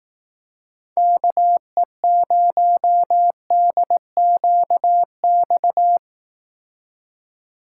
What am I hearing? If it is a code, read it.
Morse code, heard as a keyed tone, about 18 wpm, 700 Hz. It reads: KE0DQX
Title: Speed Racing - USA 2x3 Call Signs 15wpm